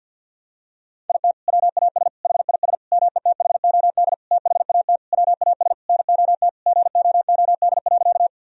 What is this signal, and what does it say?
Morse code, heard as a keyed tone, about 50 wpm, 700 Hz. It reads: IT WAS HIS METHOD THAT WAS NOT GOOD.
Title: Sentences from Top 1000 Words - Review Part 3 - 50wpm